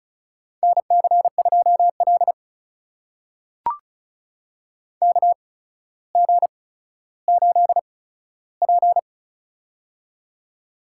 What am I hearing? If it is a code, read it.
Morse code, heard as a keyed tone, about 36 wpm, spaced as for 15 wpm, 700 Hz. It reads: NC2L KG8P
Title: USA Call Signs - 4 Characters - 8x Character Spacing - 35wpm